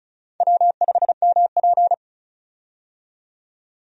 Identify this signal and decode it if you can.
Morse code, heard as a keyed tone, about 34 wpm, 700 Hz. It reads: W5MP
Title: USA Call Signs - 4 Characters - 35wpm